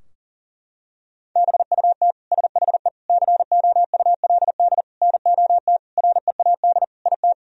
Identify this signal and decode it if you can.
Morse code, heard as a keyed tone, about 40 wpm, 700 Hz. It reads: BUT SHE COULD NOT READ IT
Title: Sentences from Top 1000 Words - Review Part 4 - 40wpm